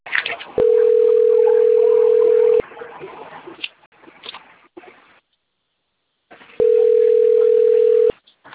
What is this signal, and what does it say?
Ringback tone (North America), 440 + 480 Hz.